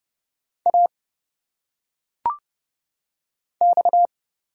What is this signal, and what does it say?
Morse code, heard as a keyed tone, about 30 wpm, 700 Hz. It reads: A X
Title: Single Letters - Phonetic Pronunciation 30wpm